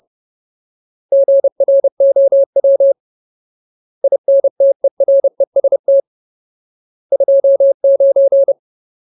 Morse code, heard as a keyed tone, about 30 wpm, 550 Hz. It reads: GROW INTEREST 29
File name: Ultimate Word List / 30WPM Ultimate